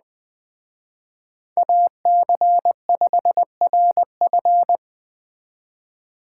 Morse code, heard as a keyed tone, about 20 wpm, 700 Hz. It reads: AC5RF